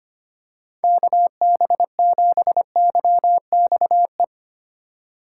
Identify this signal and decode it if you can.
Morse code, heard as a keyed tone, about 25 wpm, 700 Hz. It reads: KB7YXE
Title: USA Call Signs - 6 Characters - 2x Character Spacing - 25wpm